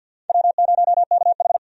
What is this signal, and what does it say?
Morse code, heard as a keyed tone, about 52 wpm, 700 Hz. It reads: W0KH
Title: USA Call Signs - 4 Characters - 50wpm